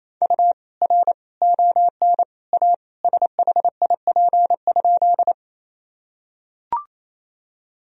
Morse code, heard as a keyed tone, about 27 wpm, 700 Hz. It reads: U R ON A SHIP?
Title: Sentences from Top 400 Words and Abbreviations 28wpm